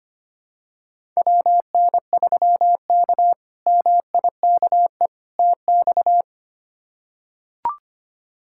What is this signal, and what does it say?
Morse code, heard as a keyed tone, about 25 wpm, 700 Hz. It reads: WN3K MIKE TX
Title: K1USN SST QSO Practice - 25wpm